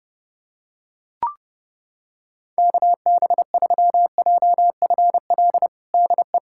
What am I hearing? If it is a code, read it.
Morse code, heard as a keyed tone, about 30 wpm, 700 Hz. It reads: KB3JFL DE
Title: USA Callsign DE Callsign 30wpm